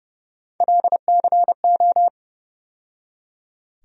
Morse code, heard as a keyed tone, about 30 wpm, 700 Hz. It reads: LCO
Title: USA Call Sign Postfixes - 3 Letters 30wpm